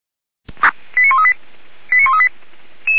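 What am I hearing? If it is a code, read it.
Five-tone selective call (ZVEI), heard as quick runs of five tones, tones of about 70 ms. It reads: 89138 89138
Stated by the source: ZVEI five-tone fire brigade alert call received by SDR.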